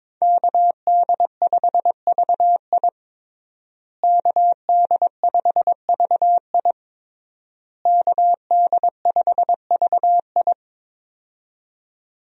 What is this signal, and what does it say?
Morse code, heard as a keyed tone, about 22 wpm, 700 Hz. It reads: KD5VI KD5VI KD5VI
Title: USA Call Signs - 5 Characters - Repeated 3x - 22wpm